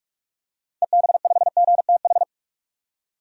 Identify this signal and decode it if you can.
Morse code, heard as a keyed tone, about 46 wpm, 700 Hz. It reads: EB5GTH
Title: DXCC-100 Call Signs - 6 Characters - 45wpm